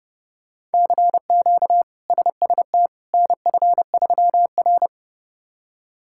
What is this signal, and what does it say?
Morse code, heard as a keyed tone, about 30 wpm, 700 Hz. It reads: CQ SST NF3R
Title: K1USN SST QSO Practice - 30wpm